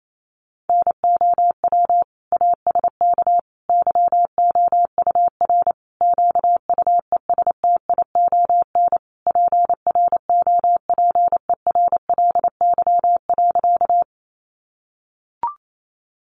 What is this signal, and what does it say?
Morse code, heard as a keyed tone, about 28 wpm, 700 Hz. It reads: NOW ASK YOUR QUESTION PROPERLY.
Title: Sentences from Top 1100 Words 28wpm